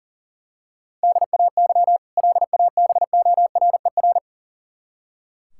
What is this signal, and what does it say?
Morse code, heard as a keyed tone, about 40 wpm, 700 Hz. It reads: DAY LABORER
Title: K1USN SST QSO Practice 40wpm